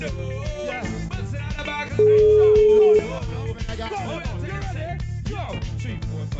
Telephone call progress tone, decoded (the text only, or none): ringback tone